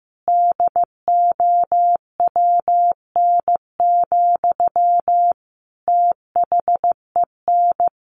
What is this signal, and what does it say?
Morse code, heard as a keyed tone, about 15 wpm, 700 Hz. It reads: DOWN, THEN
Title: Sentences from Top 1400 Words 15wpm